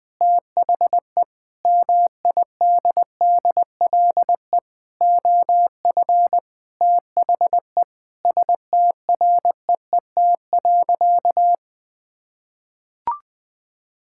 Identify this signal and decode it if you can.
Morse code, heard as a keyed tone, about 21 wpm, 700 Hz. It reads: THE MIDDLE OF THE STREET.
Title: Sentences with 4-Word Phrases - 20wpm